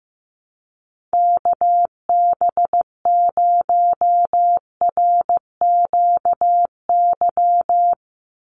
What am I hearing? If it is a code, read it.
Morse code, heard as a keyed tone, about 15 wpm, 700 Hz. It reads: KB0RQY
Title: USA Call Signs 15wpm